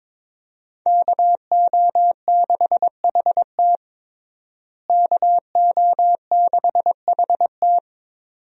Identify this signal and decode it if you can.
Morse code, heard as a keyed tone, about 22 wpm, 700 Hz. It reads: KO6HT KO6HT
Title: USA Call Signs - 5 Characters - Repeated 2x - 22wpm